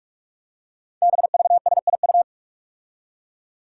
Morse code, heard as a keyed tone, about 44 wpm, 700 Hz. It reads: BVSIU
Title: Five Challenging Letter-Number Combinations 45wpm